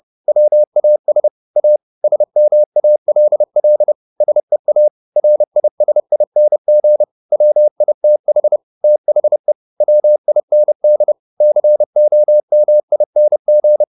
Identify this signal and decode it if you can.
Morse code, heard as a keyed tone, about 30 wpm, 600 Hz. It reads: WAS A SMALL SEA RISING WITH THE WIND COMING
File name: the old man and the sea / the old man and the sea 1 30wpm